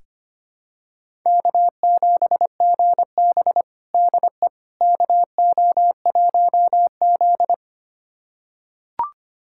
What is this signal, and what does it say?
Morse code, heard as a keyed tone, about 25 wpm, 700 Hz. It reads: K7GB DE KO1Z
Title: USA Callsign DE Callsign - 4 Characters - 25wpm